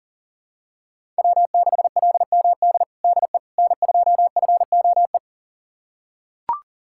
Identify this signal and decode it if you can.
Morse code, heard as a keyed tone, about 40 wpm, 700 Hz. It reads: W6LMD DE N2FOE
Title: USA Callsign DE Callsign - 5 Characters - 40wpm